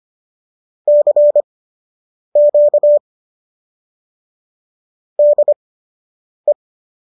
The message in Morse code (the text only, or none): CQ DE